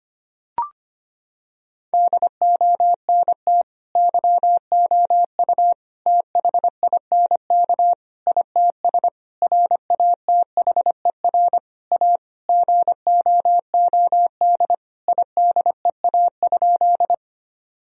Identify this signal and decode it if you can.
Morse code, heard as a keyed tone, about 25 wpm, 700 Hz. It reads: DONT YOU THINK ITS RATHER A GOOD IDEA?
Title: Sentences from Top 1000 Words - Review Part 4 - 25wpm